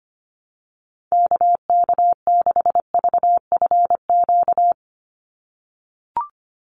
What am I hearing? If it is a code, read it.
Morse code, heard as a keyed tone, about 25 wpm, 700 Hz. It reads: KK6VFQ
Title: USA Call Signs 25wpm